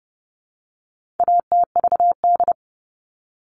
Morse code, heard as a keyed tone, about 30 wpm, 700 Hz. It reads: ATVD
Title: All Cut Numbers - 4-Digit 30wpm